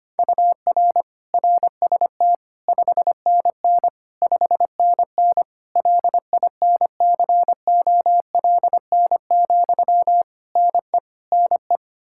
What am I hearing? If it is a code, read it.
Morse code, heard as a keyed tone, about 25 wpm, 700 Hz. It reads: UR RST 5NN 5NN LINCOLN, NE NE
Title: Practice QSO - Exchange 1 - Beginning - 25wpm